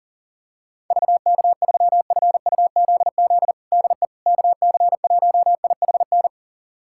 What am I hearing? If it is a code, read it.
Morse code, heard as a keyed tone, about 40 wpm, 700 Hz. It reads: VK3FUZZ DE KC1IHN
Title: USA Callsign DE Callsign 40wpm